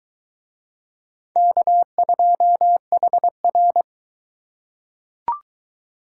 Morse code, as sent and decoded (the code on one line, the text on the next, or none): -.- ..--- .... .-.
K2HR